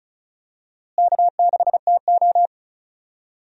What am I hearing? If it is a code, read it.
Morse code, heard as a keyed tone, about 34 wpm, 700 Hz. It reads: K6TO
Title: USA Call Signs - 4 Characters - Repeated 2x - 35wpm